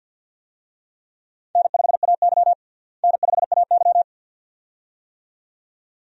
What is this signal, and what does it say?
Morse code, heard as a keyed tone, about 52 wpm, 700 Hz. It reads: N5AY N5AY
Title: USA Call Signs - 4 Characters - Repeated 2x - 50wpm